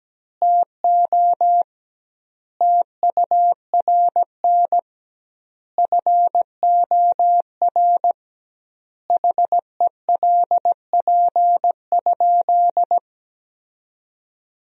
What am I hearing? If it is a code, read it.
Morse code, heard as a keyed tone, about 17 wpm, 700 Hz. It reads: TO TURN FOR HELP?
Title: 2x Word-Spacing - Sentences from Top 200 Words - 17wpm